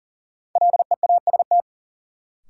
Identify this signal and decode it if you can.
Morse code, heard as a keyed tone, about 40 wpm, 700 Hz. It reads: LEAST
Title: Sentences from Top 700 Words - New Word Drill - 40wpm